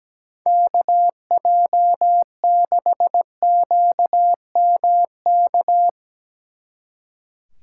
Morse code, heard as a keyed tone, about 18 wpm, 700 Hz. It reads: KJ6QMK
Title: Speed Racing - USA 2x3 Call Signs 17wpm